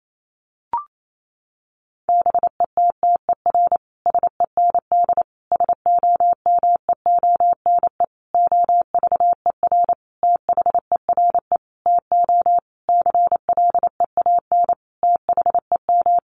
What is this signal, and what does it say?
Morse code, heard as a keyed tone, about 28 wpm, 700 Hz. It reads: BETTER SEND SOMEONE OVER THERE TO CLEAN THEM
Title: Sentences from Top 1500 Words 28wpm